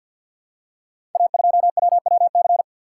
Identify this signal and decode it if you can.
Morse code, heard as a keyed tone, about 47 wpm, 700 Hz. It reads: A2WWC
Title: Code Groups - 5 Letter:Numbers - 50wpm